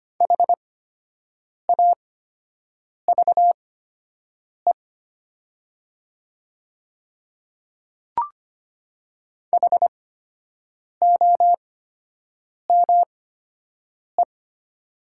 Morse code, heard as a keyed tone, about 25 wpm, spaced as for 7 wpm, 700 Hz. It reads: HAVE HOME
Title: 8x Character Spacing - Top 200 Words - 25wpm